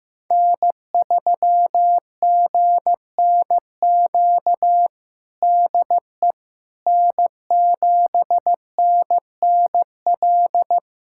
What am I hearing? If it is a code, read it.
Morse code, heard as a keyed tone, about 16 wpm, 700 Hz. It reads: N3GNQ DE N7NNL
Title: USA Callsign DE Callsign - 5 Characters - 15wpm